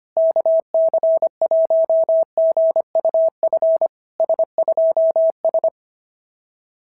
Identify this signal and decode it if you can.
Morse code, heard as a keyed tone, about 23 wpm, 650 Hz. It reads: KC1GUF S2S
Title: SOTA QSO Practice 25wpm